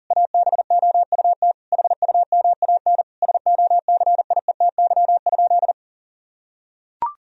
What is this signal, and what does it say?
Morse code, heard as a keyed tone, about 40 wpm, 700 Hz. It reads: ABOUT HUMAN SOCIETY?
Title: Sentences from Top 1000 Words - Review Part 4 - 40wpm